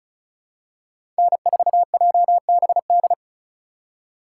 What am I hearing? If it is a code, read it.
Morse code, heard as a keyed tone, about 36 wpm, 700 Hz. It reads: N4JBD
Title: USA Call Signs - 5 Characters - 3x Character Spacing - 35wpm